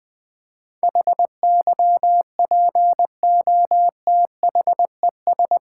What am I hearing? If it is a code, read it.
Morse code, heard as a keyed tone, about 21 wpm, 700 Hz. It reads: HYPOTHES
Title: Build A Word - 4 Letters At A Time - 10-Letter Words - 20wpm